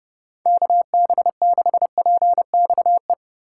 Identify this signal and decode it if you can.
Morse code, heard as a keyed tone, about 29 wpm, 700 Hz. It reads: KB6PXE